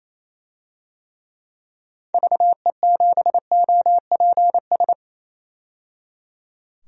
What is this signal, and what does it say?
Morse code, heard as a keyed tone, about 28 wpm, 700 Hz. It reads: VE7OPS
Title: DXCC-100 Call Signs - 6 Characters - 28wpm